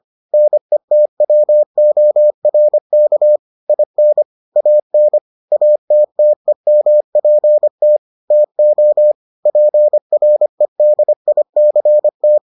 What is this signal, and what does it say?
Morse code, heard as a keyed tone, about 25 wpm, 600 Hz. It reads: NETWORK IN AN ATTEMPT TO PREDICT